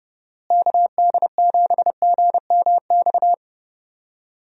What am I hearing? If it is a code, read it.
Morse code, heard as a keyed tone, about 30 wpm, 700 Hz. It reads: KD7GMX
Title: USA Call Signs - 6 Characters - Repeated 3x - 30wpm